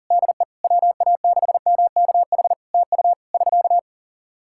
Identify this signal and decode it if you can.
Morse code, heard as a keyed tone, about 40 wpm, 700 Hz. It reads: DE WA6MKH TU <SK>
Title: Practice QSO - Exchange 4 - Ending - 40wpm